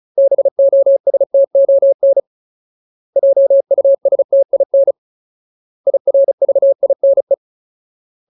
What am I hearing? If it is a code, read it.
Morse code, heard as a keyed tone, about 34 wpm, 550 Hz. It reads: BOSTON JUSTIN IRVINE
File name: Ultimate Word List / 35WPM Ultimate